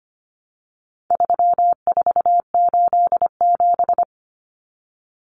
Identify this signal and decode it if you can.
Morse code, heard as a keyed tone, about 25 wpm, 700 Hz. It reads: 3487
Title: Numbers - 4-Digit 25wpm